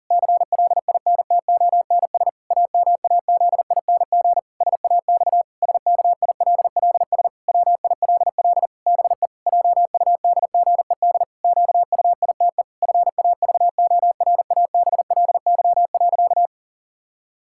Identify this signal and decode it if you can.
Morse code, heard as a keyed tone, about 40 wpm, 700 Hz. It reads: CLINTONS AMAZING SAX SKILLS WILL BE JUDGED QUITE FAVORABLY.
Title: Pangram Challenge v2 - 40wpm